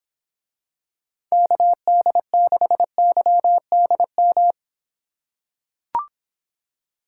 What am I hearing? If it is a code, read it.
Morse code, heard as a keyed tone, about 25 wpm, 700 Hz. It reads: KD6YDM